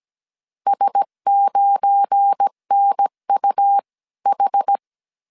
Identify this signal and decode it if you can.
Morse code, heard as a keyed tone, about 17 wpm, spaced as for 15 wpm, 800 Hz. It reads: S9NUH
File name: 1500C S B 16wpm